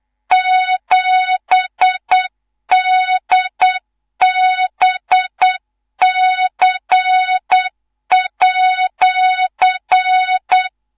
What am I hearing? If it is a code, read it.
Morse code, heard as a keyed tone, about 8 wpm, 3700 Hz, with noise. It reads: 7DBC@
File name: Morze-HA2MN / hang / 19 GYAKORLAT